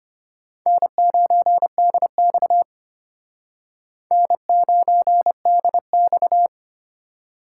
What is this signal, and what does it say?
Morse code, heard as a keyed tone, about 34 wpm, 700 Hz. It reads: N9DX N9DX